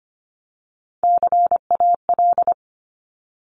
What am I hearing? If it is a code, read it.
Morse code, heard as a keyed tone, about 25 wpm, 700 Hz. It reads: CAL
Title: Top Three Letter Combinations in Words 25wpm